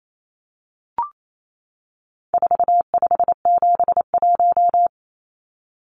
Morse code, heard as a keyed tone, about 28 wpm, 700 Hz. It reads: 4571
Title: Numbers - 4-Digit 28wpm